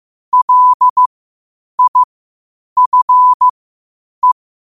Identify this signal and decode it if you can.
Morse code, heard as a keyed tone, about 15 wpm, spaced as for 8 wpm, 1000 Hz. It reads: LIFE
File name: qotdfiles / 8wpm